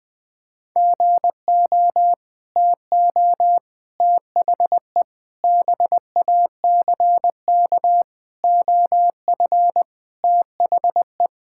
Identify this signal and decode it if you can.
Morse code, heard as a keyed tone, about 21 wpm, 700 Hz. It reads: GO TO THE BACK OF THE
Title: Sentences with 5-Word Phrases - 20wpm